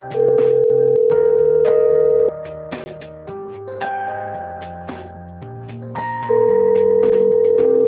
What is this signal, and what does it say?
Ringback tone (North America), 440 + 480 Hz.
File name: rg-1600-unknown- - - .25359